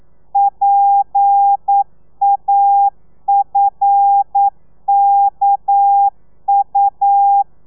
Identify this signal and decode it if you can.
Morse code, heard as a keyed tone, about 9 wpm, 800 Hz. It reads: PAFKU
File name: telegrafia / i2tfj / Lez11 40